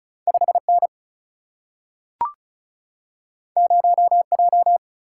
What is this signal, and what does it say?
Morse code, heard as a keyed tone, about 36 wpm, 700 Hz. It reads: 5N 0J
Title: Single-Letter Single-Number 35wpm